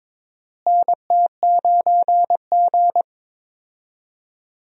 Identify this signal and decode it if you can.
Morse code, heard as a keyed tone, about 22 wpm, 700 Hz. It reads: NT9G
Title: USA Call Signs - 4 Characters - Repeated 2x - 22wpm